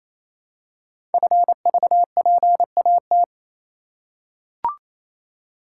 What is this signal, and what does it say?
Morse code, heard as a keyed tone, about 27 wpm, 700 Hz. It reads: FVPAT
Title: Code Groups - 5 Letters - 28wpm